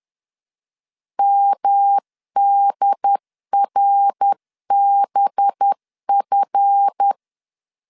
Morse code, heard as a keyed tone, about 10 wpm, 800 Hz. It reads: MDRBF
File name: LEZIONI / 7 GF9Z / 7 GF 10wpm 2514